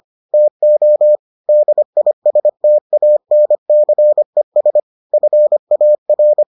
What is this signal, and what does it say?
Morse code, heard as a keyed tone, about 25 wpm, 600 Hz. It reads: TO DISTANCES FAR